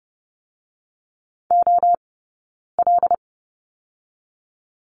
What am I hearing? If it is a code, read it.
Morse code, heard as a keyed tone, about 30 wpm, 700 Hz. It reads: O L
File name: Lesson 17d - Letter u - Two Characters 30wpm